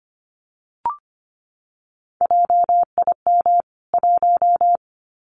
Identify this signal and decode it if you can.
Morse code, heard as a keyed tone, about 25 wpm, 700 Hz. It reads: JIM 1
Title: CWT Exchange 25wpm